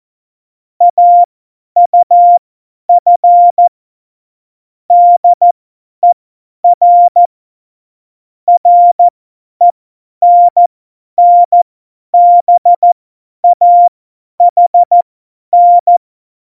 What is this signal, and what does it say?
Morse code, heard as a keyed tone, about 14 wpm, spaced as for 10 wpm, 700 Hz. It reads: AUF DER RENNBAHN
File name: Morsekurs / Witze / Witz 11 August 2020, 50 F70